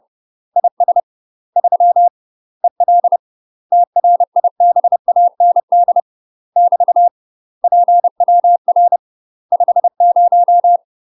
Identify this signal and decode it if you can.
Morse code, heard as a keyed tone, about 30 wpm, 700 Hz. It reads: IS 3 EL TRIBAND <BT> PWR 50